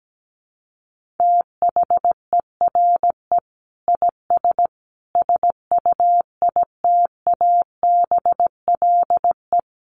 Morse code, heard as a keyed tone, about 17 wpm, 700 Hz. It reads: THERE IS SUITABLE
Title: Sentences from Top 1500 Words 17wpm